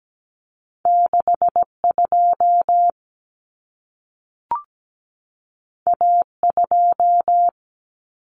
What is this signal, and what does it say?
Morse code, heard as a keyed tone, about 17 wpm, 700 Hz. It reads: 62 A2
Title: Common Cut Numbers - 2-Digit 17wpm